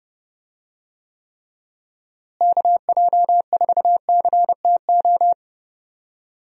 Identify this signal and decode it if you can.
Morse code, heard as a keyed tone, about 30 wpm, 700 Hz. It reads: KJ4CTO
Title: USA Call Signs - 6 Characters - Repeated 2x - 30wpm